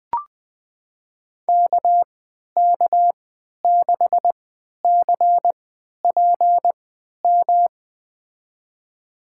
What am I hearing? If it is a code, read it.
Morse code, heard as a keyed tone, about 20 wpm, spaced as for 11 wpm, 700 Hz. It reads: KK6CPM
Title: USA Call Signs - 3x Character Spacing - 20wpm